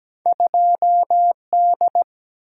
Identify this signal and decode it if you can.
Morse code, heard as a keyed tone, about 17 wpm, 700 Hz. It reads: 2D
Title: Single-Letter Single-Number 17wpm